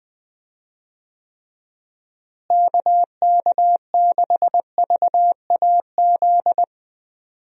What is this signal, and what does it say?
Morse code, heard as a keyed tone, about 20 wpm, 700 Hz. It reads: KK6VAZ
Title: USA Call Signs - 6 Characters - Repeated 3x - 20wpm